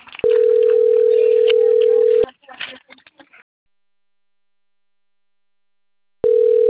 Ringback tone (North America), 440 + 480 Hz.
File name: rg-1600-unknown- - - .63979